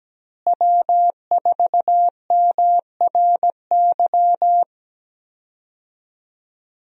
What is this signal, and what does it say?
Morse code, heard as a keyed tone, about 17 wpm, 700 Hz. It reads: W4MRY
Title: USA Call Signs - 5 Characters - Repeated 2x - 17wpm